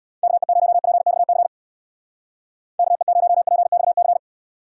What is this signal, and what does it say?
Morse code, heard as a keyed tone, about 60 wpm, 700 Hz. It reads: KE0PXC KE0PXC
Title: Speed Racing - USA 2x3 Call Signs 50wpm